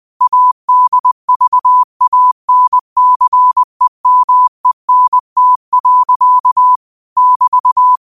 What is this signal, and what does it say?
Morse code, heard as a keyed tone, about 20 wpm, 1000 Hz. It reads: ADVANCEMENT. <BT>
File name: qotdfiles / 20wpm